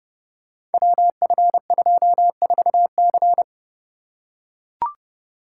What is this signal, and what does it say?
Morse code, heard as a keyed tone, about 29 wpm, 700 Hz. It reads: WF24C
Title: Code Groups - 5 Letter:Numbers - 30wpm